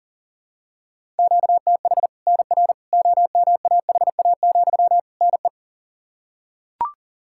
Morse code, heard as a keyed tone, about 40 wpm, 700 Hz. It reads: QTH NR OMAHA, NE
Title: Practice QSO - Exchange 1 - Beginning - 40wpm